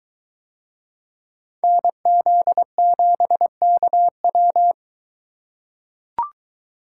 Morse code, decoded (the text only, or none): NZ7KW